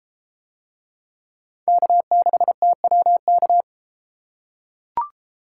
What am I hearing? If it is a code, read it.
Morse code, heard as a keyed tone, about 31 wpm, 700 Hz. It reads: K6TWK